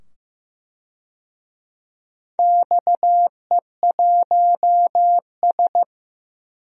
Morse code, heard as a keyed tone, about 15 wpm, 700 Hz. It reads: XE1S
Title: DXCC-100 Call Signs - 4 Characters - 15wpm